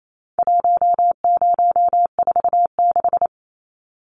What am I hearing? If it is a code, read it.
Morse code, heard as a keyed tone, about 28 wpm, 700 Hz. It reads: 1046
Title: Numbers - 4-Digit 28wpm